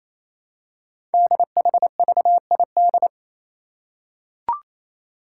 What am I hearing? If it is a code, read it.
Morse code, heard as a keyed tone, about 28 wpm, 700 Hz. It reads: DHVID
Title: Code Groups - 5 Letters - 28wpm